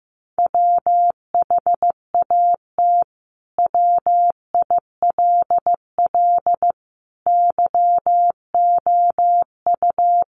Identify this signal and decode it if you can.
Morse code, heard as a keyed tone, about 15 wpm, 700 Hz. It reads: WHAT WILL YOU
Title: Sentences from Top 1100 Words 15wpm